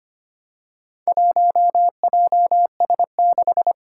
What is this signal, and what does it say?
Morse code, heard as a keyed tone, about 25 wpm, 700 Hz. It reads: 1JS6
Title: Four Challenging Letter-Number Combinations 25wpm